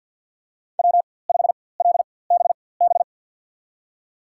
Morse code, heard as a keyed tone, about 52 wpm, spaced as for 24 wpm, 700 Hz. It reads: W5LBB
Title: USA Call Signs - 5 Characters - 4x Character Spacing - 50wpm